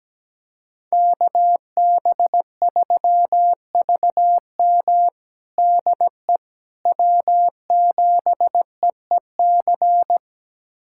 Morse code, heard as a keyed tone, about 17 wpm, 700 Hz. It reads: KB3VM DE W7EEC
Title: USA Callsign DE Callsign - 5 Characters - 17wpm